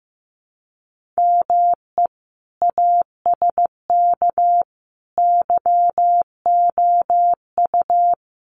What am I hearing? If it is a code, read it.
Morse code, heard as a keyed tone, about 15 wpm, 700 Hz. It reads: ME ASK YOU
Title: Sets of 5 Words Encore2 15wpm